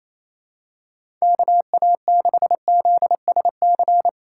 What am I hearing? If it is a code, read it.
Morse code, heard as a keyed tone, about 28 wpm, 700 Hz. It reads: KA6ZSC
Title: USA Call Signs - 4x Character Spacing - 28wpm